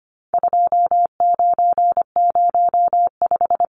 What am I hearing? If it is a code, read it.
Morse code, heard as a keyed tone, about 25 wpm, 700 Hz. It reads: 2905
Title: Numbers - 4-Digit 25wpm